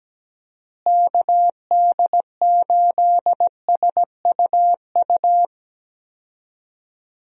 Morse code, heard as a keyed tone, about 17 wpm, 700 Hz. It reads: KD8SUU
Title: USA Call Signs - 6 Characters - Repeated 2x - 17wpm